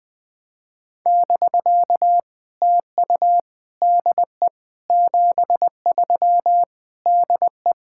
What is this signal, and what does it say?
Morse code, heard as a keyed tone, about 21 wpm, 700 Hz. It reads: <BK> TU DE 73 DE
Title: POTA QSO Practice - 20wpm